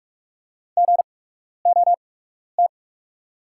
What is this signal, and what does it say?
Morse code, heard as a keyed tone, about 45 wpm, spaced as for 12 wpm, 690 Hz. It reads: GOT
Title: 8x Character Spacing - Top 300 Words - 45wpm